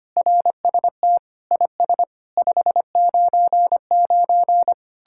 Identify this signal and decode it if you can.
Morse code, heard as a keyed tone, about 31 wpm, 700 Hz. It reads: RST IS 599
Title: Speed-Racing Common QSO Phrases - 25wpm